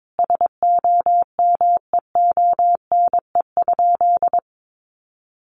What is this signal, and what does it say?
Morse code, heard as a keyed tone, about 22 wpm, 700 Hz. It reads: SOMEONE?
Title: Sentences from Top 600 Words 22wpm